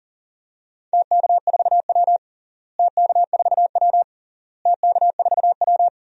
Morse code, heard as a keyed tone, about 40 wpm, 700 Hz. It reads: TK4W TK4W TK4W
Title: DXCC-100 Call Signs - 4 Characters - Repeated 3x - 40wpm